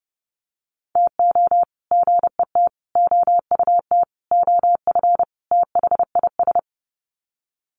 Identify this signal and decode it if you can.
Morse code, heard as a keyed tone, about 30 wpm, 700 Hz. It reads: TO GET OUT OF THIS
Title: Sets of 5 Words Encore2 30wpm